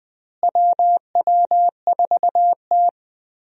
Morse code, heard as a keyed tone, about 20 wpm, 700 Hz. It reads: WW4T
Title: USA Call Signs - 4 Characters - Repeated 3x - 20wpm